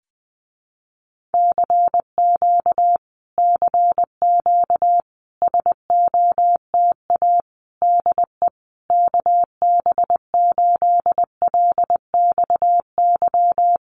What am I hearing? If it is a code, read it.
Morse code, heard as a keyed tone, about 20 wpm, 700 Hz. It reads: CQ CQ SOTA DE KB8LXY
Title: Common QSO Phrases Encore 20wpm